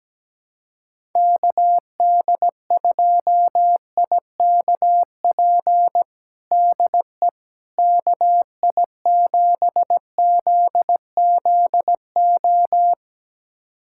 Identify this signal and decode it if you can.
Morse code, heard as a keyed tone, about 17 wpm, 700 Hz. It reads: KD2IKP DE KI7ZZO
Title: USA Callsign DE Callsign - 6 Characters - 17wpm